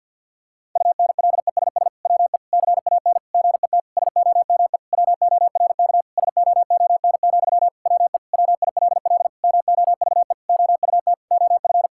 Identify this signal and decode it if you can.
Morse code, heard as a keyed tone, about 52 wpm, 700 Hz. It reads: UNLESS WE CAN GET SOME WORK SOON, WE WILL MOVE OUT OF
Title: Sentences from Top 1000 Words - Review Part 6 - 50wpm